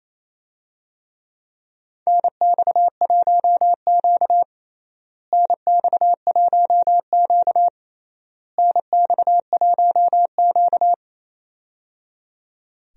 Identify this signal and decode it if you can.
Morse code, heard as a keyed tone, about 28 wpm, 700 Hz. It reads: NX1Q NX1Q NX1Q
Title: USA Call Signs - 4 Characters - Repeated 3x - 28wpm